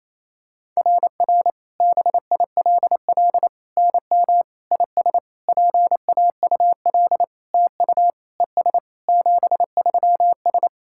Morse code, heard as a keyed tone, about 28 wpm, 690 Hz. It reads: RR BILL NM IS PAUL TU ES 73S
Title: SOTA QSO Practice 28wpm